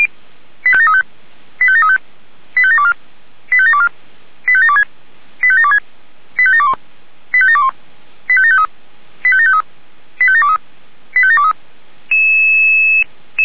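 Five-tone selective call (ZVEI), heard as quick runs of five tones, tones of about 70 ms. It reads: 85735 85735 85724 85724 85726 85726 85721 85721 85753 85753 85723 85723
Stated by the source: ZVEI five-tone call 85741 received by SDR.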